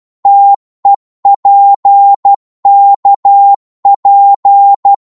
Morse code, heard as a keyed tone, about 12 wpm, 800 Hz. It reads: TEPKP
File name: Morse / ltr-12wpm